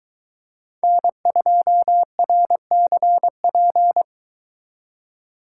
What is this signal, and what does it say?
Morse code, heard as a keyed tone, about 23 wpm, 700 Hz. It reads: N2RCP